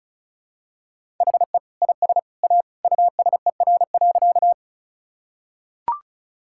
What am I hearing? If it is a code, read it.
Morse code, heard as a keyed tone, about 34 wpm, 700 Hz. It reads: HE IS A USER.
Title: Sentences from Top 1000 Words - Review Part 5 - 35wpm